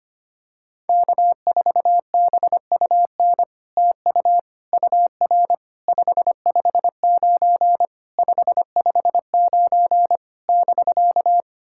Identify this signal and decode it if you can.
Morse code, heard as a keyed tone, about 25 wpm, 700 Hz. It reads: K4BUN TU UR 559 559 <BK>
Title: SOTA QSO Practice 25wpm